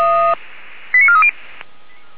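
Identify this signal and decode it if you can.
Five-tone selective call (ZVEI), heard as quick runs of five tones, tones of about 70 ms. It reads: 89430
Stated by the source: ZVEI five-tone selective call 89330, received on SDR.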